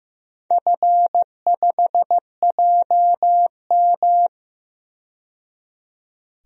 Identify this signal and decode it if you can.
Morse code, heard as a keyed tone, about 15 wpm, 700 Hz. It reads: F5JM
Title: DXCC-100 Call Signs - 4 Characters - 15wpm